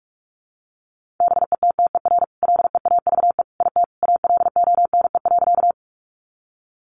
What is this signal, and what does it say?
Morse code, heard as a keyed tone, about 44 wpm, 700 Hz. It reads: BETTER LEAVE IT ALONE.
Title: Sentences from Top 1000 Words 45wpm